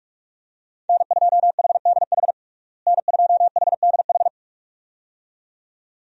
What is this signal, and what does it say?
Morse code, heard as a keyed tone, about 42 wpm, 700 Hz. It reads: N2HDH N2HDH
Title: USA Call Signs - 5 Characters - Repeated 2x - 45wpm